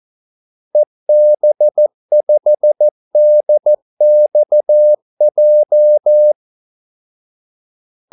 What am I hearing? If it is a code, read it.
Morse code, heard as a keyed tone, about 14 wpm, 600 Hz. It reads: EB5DXJ
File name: Callsigns-14wpm